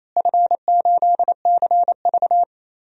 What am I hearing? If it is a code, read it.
Morse code, heard as a keyed tone, about 27 wpm, 700 Hz. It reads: F8CV
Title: DXCC-100 Call Signs - 4 Characters - Repeated 3x - 28wpm